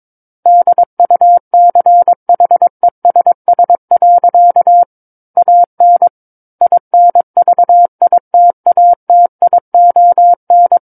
Morse code, heard as a keyed tone, about 22 wpm, 700 Hz. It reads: DUCHESS. AN INVITATION
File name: 06-Chapter-06 22 WPM